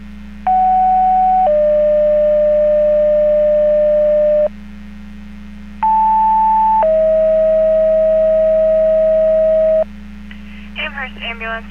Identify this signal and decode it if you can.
Two-tone page, heard as a one-second tone, then a three-second tone, about 750 then 600 Hz, about 880 then 640 Hz, with speech. It is heard from 0.5 to 9.8 s.